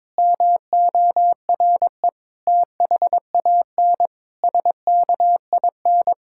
Morse code, heard as a keyed tone, about 22 wpm, 700 Hz. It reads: MORE THAN SKIN
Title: Sentences from Top 1600 Words 22wpm